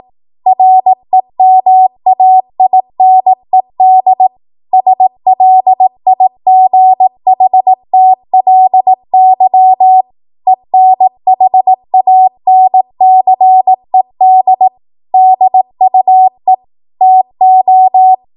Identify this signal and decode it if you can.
Morse code, heard as a keyed tone, about 18 wpm, 750 Hz. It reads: REMAINED SLIGHTLY ENHANCED DUE TO
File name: Morse / 18WPM